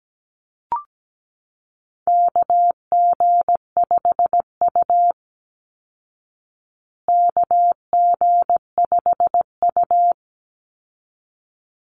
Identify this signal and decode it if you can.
Morse code, heard as a keyed tone, about 17 wpm, 700 Hz. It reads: KG5U KG5U
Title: Call Signs Repeated 3x - 17wpm